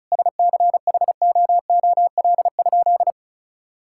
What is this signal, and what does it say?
Morse code, heard as a keyed tone, about 34 wpm, 700 Hz. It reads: SCHOOL?
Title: Sentences from Top 1000 Words - Review Part 6 - 35wpm